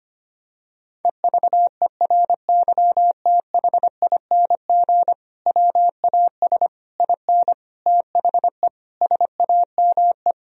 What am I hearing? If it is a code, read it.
Morse code, heard as a keyed tone, about 25 wpm, 700 Hz. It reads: EVERYTHING WAS IN THE SAME
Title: Sentences from Top 1000 Words - Review Part 2 - 25wpm